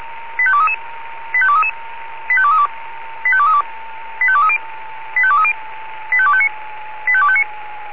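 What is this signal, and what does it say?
Five-tone selective call (ZVEI), heard as quick runs of five tones, tones of about 70 ms. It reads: 85233 85233 85232 85232 85230 85230 85259 85259
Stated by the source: ZVEI five-tone selective call received on an SDR.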